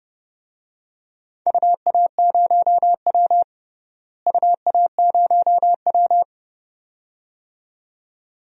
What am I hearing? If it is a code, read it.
Morse code, heard as a keyed tone, about 30 wpm, 700 Hz. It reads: UA0W UA0W
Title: DXCC-100 Call Signs - 4 Characters - Repeated 2x - 30wpm